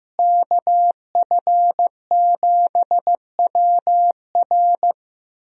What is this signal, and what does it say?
Morse code, heard as a keyed tone, about 15 wpm, 700 Hz. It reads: KF7WR
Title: USA Call Signs - 5 Characters - Repeated 3x - 15wpm